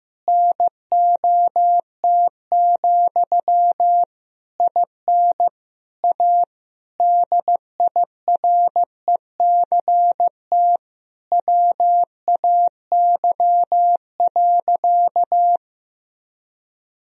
Morse code, heard as a keyed tone, about 15 wpm, 700 Hz. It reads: NOT, IN A DIRECT WAY.
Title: Sentences from Top 1000 Words - Review Part 2 - 15wpm